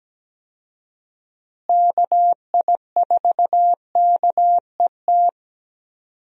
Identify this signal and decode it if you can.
Morse code, heard as a keyed tone, about 17 wpm, 700 Hz. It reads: KI4KET
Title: USA Call Signs - 6 Characters - Repeated 3x - 17wpm